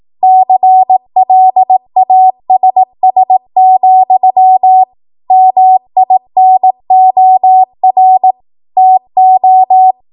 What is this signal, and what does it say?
Morse code, heard as a keyed tone, about 18 wpm, 750 Hz. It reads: CLASS, MINOR TO
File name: Morse / 18WPM